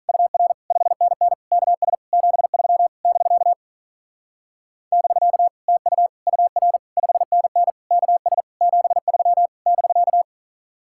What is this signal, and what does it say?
Morse code, heard as a keyed tone, about 44 wpm, 700 Hz. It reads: UR 5NN KS 73 <BK> <BK> TU UR 5NN KS 73 <BK>